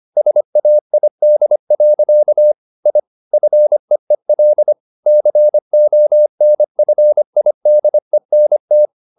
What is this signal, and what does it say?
Morse code, heard as a keyed tone, about 25 wpm, 600 Hz. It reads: SAID. I FEEL CONFIDENT
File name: the old man and the sea part 1 25wpm